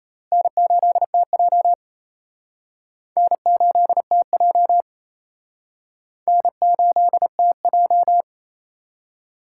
Morse code, heard as a keyed tone, about 31 wpm, 700 Hz. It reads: N8TJ N8TJ N8TJ
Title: Speed Racing - USA 1x2 Call Signs 28wpm